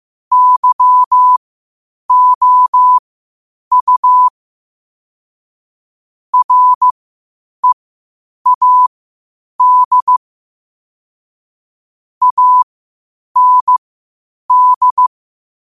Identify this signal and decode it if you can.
Morse code, heard as a keyed tone, about 15 wpm, spaced as for 8 wpm, 1000 Hz. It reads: YOU READ AND
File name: qotdfiles / 8wpm